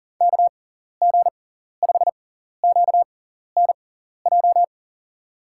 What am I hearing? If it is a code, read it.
Morse code, heard as a keyed tone, about 40 wpm, spaced as for 14 wpm, 700 Hz. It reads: KG5QNJ
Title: USA Call Signs - 6 Characters - 6x Character Spacing - 40wpm